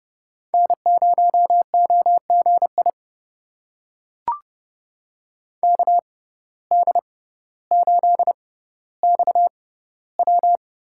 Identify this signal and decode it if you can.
Morse code, heard as a keyed tone, about 29 wpm, spaced as for 15 wpm, 700 Hz. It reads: N0OGI KD8XW
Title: USA Call Signs - 5 Characters - 6x Character Spacing - 30wpm